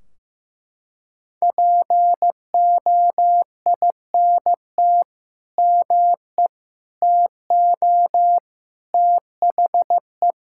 Morse code, heard as a keyed tone, about 16 wpm, 700 Hz. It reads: POINT ME TO THE
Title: Sentences from Top 1600 Words 15wpm